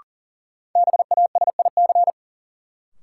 Morse code, heard as a keyed tone, about 40 wpm, 700 Hz. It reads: BASIC
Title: Sentences from Top 1000 Words - Review Part 3 - New Word Drill - 40wpm